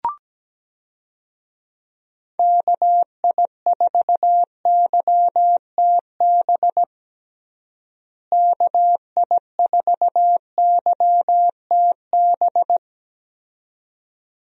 Morse code, heard as a keyed tone, about 17 wpm, 700 Hz. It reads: KI4YTB KI4YTB
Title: USA Call Signs - 6 Characters - Repeated 2x - 17wpm